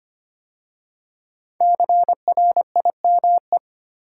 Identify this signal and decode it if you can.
Morse code, heard as a keyed tone, about 25 wpm, 700 Hz. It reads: CRIME
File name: 5 Letter Words - First 3 Letters Given - 25wpm